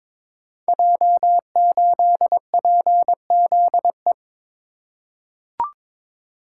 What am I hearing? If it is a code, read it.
Morse code, heard as a keyed tone, about 22 wpm, 700 Hz. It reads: J8PZE
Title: Code Groups - 5 Letter:Numbers - 22wpm